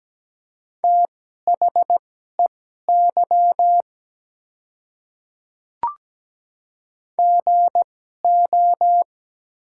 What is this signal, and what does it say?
Morse code, heard as a keyed tone, about 17 wpm, spaced as for 12 wpm, 700 Hz. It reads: THEY GO
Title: 2x Character Spacing - Top 200 Words - 17wpm